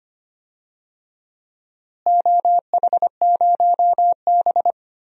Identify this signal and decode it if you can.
Morse code, heard as a keyed tone, about 25 wpm, 700 Hz. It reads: OH0B
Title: DXCC-100 Call Signs - 4 Characters - Repeated 3x - 25wpm